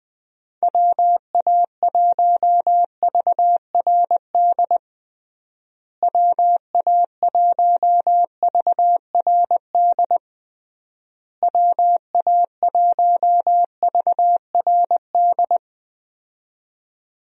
Morse code, heard as a keyed tone, about 21 wpm, 700 Hz. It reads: WA1VRD WA1VRD WA1VRD
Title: USA Call Signs - 6 Characters - Repeated 3x - 20wpm